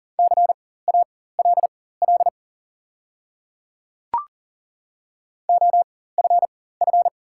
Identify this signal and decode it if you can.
Morse code, heard as a keyed tone, about 38 wpm, spaced as for 18 wpm, 700 Hz. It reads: CALL OFF
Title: 4x Character Spacing - Top 200 Words - 40wpm